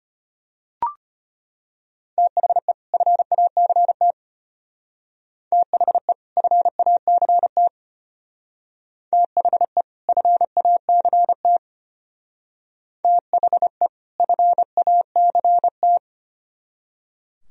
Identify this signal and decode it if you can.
Morse code, heard as a keyed tone, about 30 wpm, 700 Hz. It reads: THE FACT THE FACT THE FACT THE FACT
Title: Speed-Racing - Sets of 2 Words 25wpm